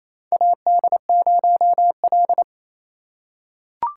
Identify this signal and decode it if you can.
Morse code, heard as a keyed tone, about 27 wpm, 700 Hz. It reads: AD0L
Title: USA Call Signs - 4 Characters - 6x Character Spacing - 28wpm